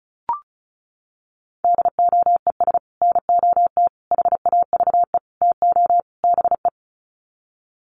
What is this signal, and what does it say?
Morse code, heard as a keyed tone, about 34 wpm, 700 Hz. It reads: DOES NOT HAVE TO BE
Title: Sets of 5 Words 35wpm